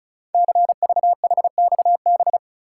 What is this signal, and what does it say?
Morse code, heard as a keyed tone, about 36 wpm, 690 Hz. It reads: CVHXB
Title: Code Groups - 5 Letters - 35wpm